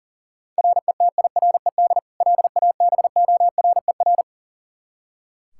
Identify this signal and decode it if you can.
Morse code, heard as a keyed tone, about 37 wpm, 700 Hz. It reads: RETIRED LABORER
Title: Practice QSO - Exchange 3 - More Info - 40wpm